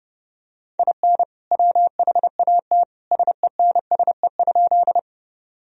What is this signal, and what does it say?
Morse code, heard as a keyed tone, about 30 wpm, 700 Hz. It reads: IN WHAT SENSE?
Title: Sentences from Top 1000 Words - Review Part 4 - 30wpm